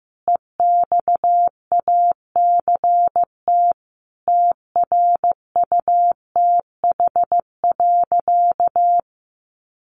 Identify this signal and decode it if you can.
Morse code, heard as a keyed tone, about 15 wpm, 700 Hz. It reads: EXACT TRUTH.
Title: Sentences from Top 1300 Words 15wpm